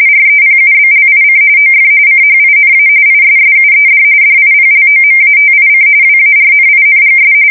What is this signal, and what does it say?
RTTY, about 45.45 baud. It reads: REPUBLIC OF CHINA ISSUED DIPLOMATIC WARNINGS
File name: fsk170 / USS-Missouri